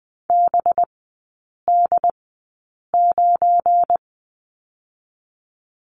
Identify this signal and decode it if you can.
Morse code, heard as a keyed tone, about 20 wpm, spaced as for 8 wpm, 700 Hz. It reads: BD9
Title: 184 Lesson 30e - Letter b - Three Characters 20wpm